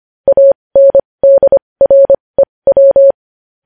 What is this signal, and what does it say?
Morse code, heard as a keyed tone, about 25 wpm, spaced as for 20 wpm, 550 Hz. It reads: ANDREW
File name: podcast / News headlines-1x-25x20